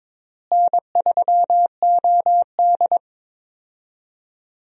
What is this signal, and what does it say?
Morse code, heard as a keyed tone, about 22 wpm, 700 Hz. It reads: N3OD
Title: USA Call Signs - 4 Characters - Repeated 2x - 22wpm